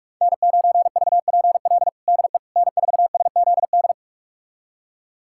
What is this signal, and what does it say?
Morse code, heard as a keyed tone, about 41 wpm, 700 Hz. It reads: N9VPL DE N4SZD
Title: USA Callsign DE Callsign - 5 Characters - 45wpm